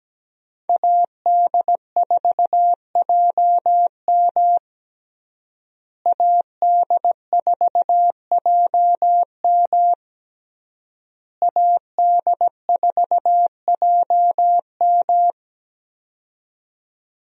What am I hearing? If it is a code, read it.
Morse code, heard as a keyed tone, about 17 wpm, 700 Hz. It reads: AD4JM AD4JM AD4JM
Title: USA Call Signs - 5 Characters - Repeated 3x - 17wpm